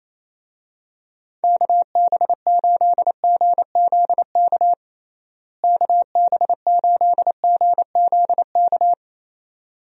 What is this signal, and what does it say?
Morse code, heard as a keyed tone, about 27 wpm, 700 Hz. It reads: KB8GZK KB8GZK
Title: USA Call Signs - 6 Characters - Repeated 2x - 28wpm